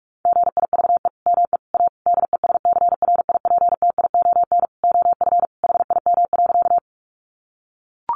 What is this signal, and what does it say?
Morse code, heard as a keyed tone, about 44 wpm, 700 Hz. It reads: GIVE ME A DESCRIPTION OF HIM.
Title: Sentences from Top 1300 Words 45wpm